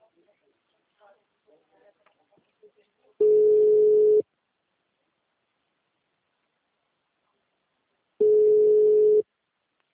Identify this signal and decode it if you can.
Ringback tone (Europe, 425 Hz), with speech.